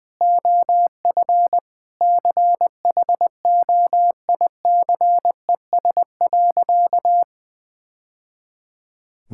Morse code, heard as a keyed tone, about 20 wpm, 700 Hz. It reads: OF CHOICES.
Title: Sentences with 3-Word Phrases - 20wpm